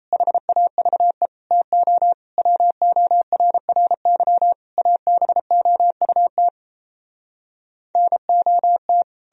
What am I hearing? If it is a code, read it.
Morse code, heard as a keyed tone, about 37 wpm, 700 Hz. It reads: HAVE TO WORRY ABOUT NOT